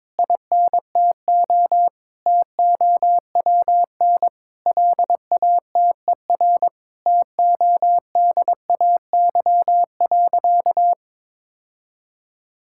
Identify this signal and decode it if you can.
Morse code, heard as a keyed tone, about 22 wpm, 700 Hz. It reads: INTO TOWN LATER TODAY.
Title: Sentences from Top 1000 Words - Review Part 6 - 22wpm